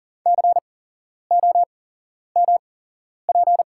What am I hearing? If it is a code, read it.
Morse code, heard as a keyed tone, about 40 wpm, spaced as for 11 wpm, 690 Hz. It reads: COMP
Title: 8x Character Spacing - Top 500 Words - 40wpm